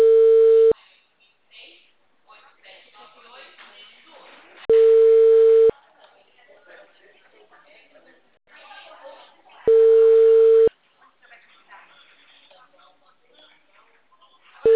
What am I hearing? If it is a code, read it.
Ringback tone (China, 450 Hz), with speech.
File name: out- -2015- - - .13363